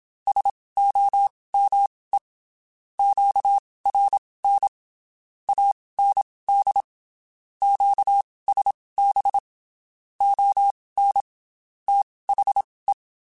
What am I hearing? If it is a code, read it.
Morse code, heard as a keyed tone, about 27 wpm, spaced as for 19 wpm, 800 Hz. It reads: SOME QRN AND QSB ON THE
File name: QSOs / QSOs-mp3s / WD0FBT 1@25